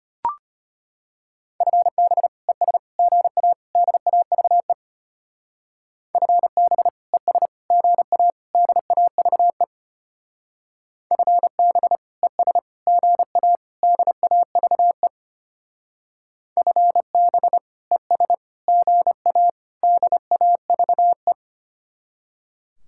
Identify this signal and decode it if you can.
Morse code, heard as a keyed tone, about 30 wpm, 700 Hz. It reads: FB ES GA DAVE FB ES GA DAVE FB ES GA DAVE FB ES GA DAVE
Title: Speed-Racing Common QSO Phrases - 25wpm